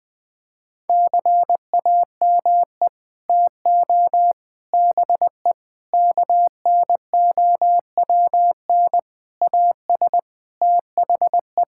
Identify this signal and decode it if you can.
Morse code, heard as a keyed tone, about 19 wpm, 700 Hz. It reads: CAME TO BE KNOWN AS THE
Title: Sets of 6 Words - 20wpm